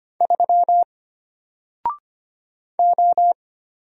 Morse code, heard as a keyed tone, about 25 wpm, 700 Hz. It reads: 3 O
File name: Single-Letter-Number-25wpm